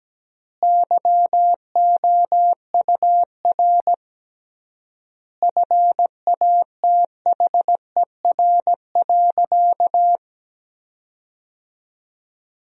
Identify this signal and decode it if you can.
Morse code, heard as a keyed tone, about 17 wpm, 700 Hz. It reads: YOUR FATHER.
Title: 3x Word-Spacing - Sentences from Top 300 Words - 17wpm